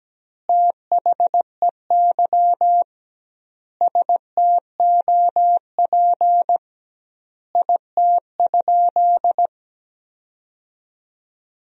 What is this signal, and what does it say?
Morse code, heard as a keyed tone, about 17 wpm, 700 Hz. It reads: THEY STOP IT?
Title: 2x Word-Spacing - Sentences from Top 300 Words - 17wpm